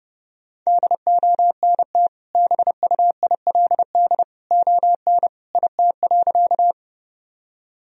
Morse code, heard as a keyed tone, about 32 wpm, 700 Hz. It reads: DONT BUILD ON IT.
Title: Sentences from Top 1000 Words - Review Part 1 - 30wpm